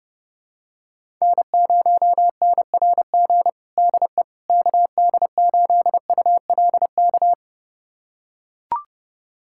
Morse code, heard as a keyed tone, about 30 wpm, 700 Hz. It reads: N0NRG DE KD8ULK
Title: USA Callsign DE Callsign 30wpm